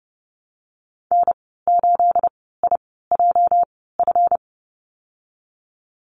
Morse code, heard as a keyed tone, about 30 wpm, spaced as for 17 wpm, 700 Hz. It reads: N8IJF
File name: Lesson 37g - Letter x - Call Signs 30wpm